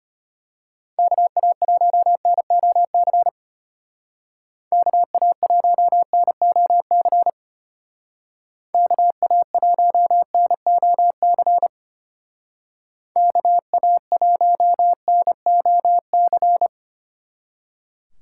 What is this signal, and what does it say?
Morse code, heard as a keyed tone, about 30 wpm, 700 Hz. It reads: KA1NOC KA1NOC KA1NOC KA1NOC
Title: Speed Racing - USA 2x3 Call Signs 25wpm